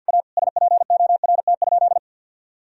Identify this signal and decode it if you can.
Morse code, heard as a keyed tone, about 50 wpm, 700 Hz. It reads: A SPORT?
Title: Sentences from Top 1000 Words - Review Part 4 - 50wpm